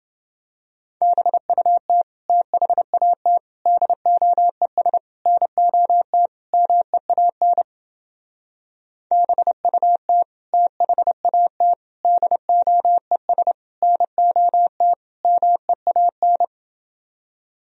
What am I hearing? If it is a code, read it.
Morse code, heard as a keyed tone, about 31 wpm, 700 Hz. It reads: BUT THAT DOES NOT MEAN BUT THAT DOES NOT MEAN